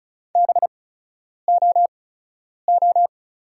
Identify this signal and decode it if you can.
Morse code, heard as a keyed tone, about 34 wpm, spaced as for 10 wpm, 700 Hz. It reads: BOO
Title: 8x Character Spacing - Top 300 Words - 35wpm